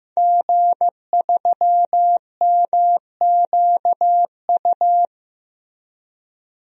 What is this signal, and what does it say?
Morse code, heard as a keyed tone, about 15 wpm, 700 Hz. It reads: G3MQU
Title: DXCC-100 Call Signs - 5 Characters - 15wpm